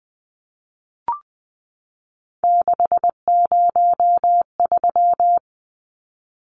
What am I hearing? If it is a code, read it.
Morse code, heard as a keyed tone, about 20 wpm, 700 Hz. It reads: 603
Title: Numbers - 3-Digit 20wpm